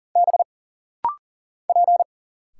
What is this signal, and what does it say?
Morse code, heard as a keyed tone, about 40 wpm, 700 Hz. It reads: B P
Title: Single Letters - Rapid-Fire 40wpm